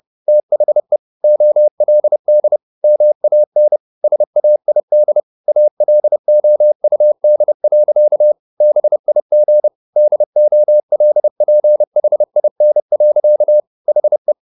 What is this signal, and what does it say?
Morse code, heard as a keyed tone, about 30 wpm, 600 Hz. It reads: THE OLD MAN SAID ALOUD. BIG DOLPHIN. HE
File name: the old man and the sea / the old man and the sea 1 30wpm